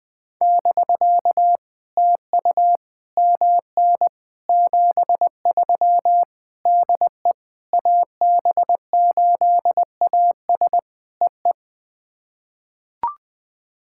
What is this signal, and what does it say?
Morse code, heard as a keyed tone, about 21 wpm, 700 Hz. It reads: <BK> TU MN 73 DE AB8AS EE
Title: POTA QSO Practice - 20wpm